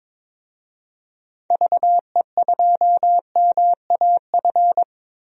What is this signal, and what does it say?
Morse code, heard as a keyed tone, about 22 wpm, 700 Hz. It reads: VE2MAF
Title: DXCC-100 Call Signs - 6 Characters - Repeated 3x - 22wpm